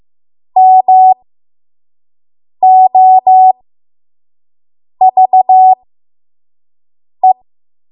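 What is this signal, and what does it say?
Morse code, heard as a keyed tone, about 15 wpm, spaced as for 5 wpm, 750 Hz. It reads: MOVE
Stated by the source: 5 WPM Morse code practice.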